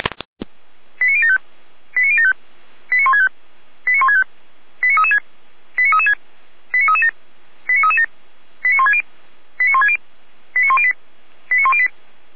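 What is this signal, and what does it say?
Five-tone selective call (ZVEI), heard as quick runs of five tones, tones of about 70 ms. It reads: 89075 89075 89165 89165 89336 89336 89337 89337 89160 89160 89108 89108
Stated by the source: ZVEI five-tone selective call received on an SDR.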